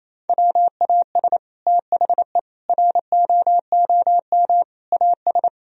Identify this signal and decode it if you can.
Morse code, heard as a keyed tone, about 28 wpm, 700 Hz. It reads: WAS THE ROOM AS
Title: Sentences from Top 1000 Words - Review Part 4 - 28wpm